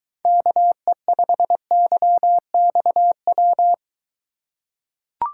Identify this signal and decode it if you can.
Morse code, heard as a keyed tone, about 23 wpm, 700 Hz. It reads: KE5YXW